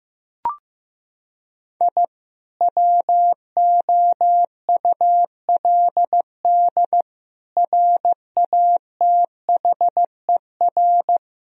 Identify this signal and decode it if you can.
Morse code, heard as a keyed tone, about 15 wpm, 700 Hz. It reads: I WOULD RATHER
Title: Sentences from Top 1000 Words - Review Part 4 - 15wpm